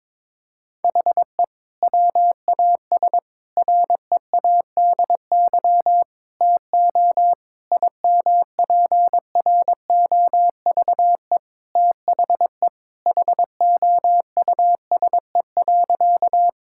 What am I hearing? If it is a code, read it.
Morse code, heard as a keyed tone, about 22 wpm, 700 Hz. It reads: HE WAS READY TO IMPROVE THE HOUSE.
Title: Sentences from Top 1000 Words - Review Part 5 - 22wpm